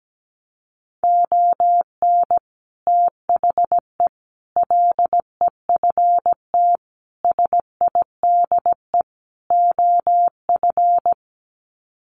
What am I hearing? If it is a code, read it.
Morse code, heard as a keyed tone, about 17 wpm, 700 Hz. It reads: ON THE LEFT SIDE OF
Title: Sets of 5 Words Encore2 17wpm